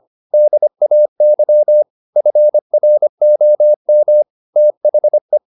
Morse code, heard as a keyed tone, about 25 wpm, 600 Hz. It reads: DAY FROM THE
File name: the old man and the sea part 1 25wpm